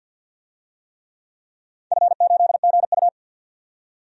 Morse code, heard as a keyed tone, about 52 wpm, 700 Hz. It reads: F8GU
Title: DXCC-100 Call Signs - 4 Characters - 50wpm